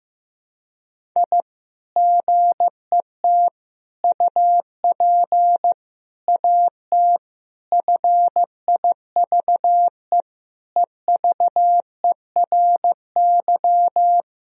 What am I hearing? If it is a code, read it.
Morse code, heard as a keyed tone, about 16 wpm, 700 Hz. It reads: I GET UP AT FIVE EVERY
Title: Sentences from Top 1000 Words - Review Part 2 - 15wpm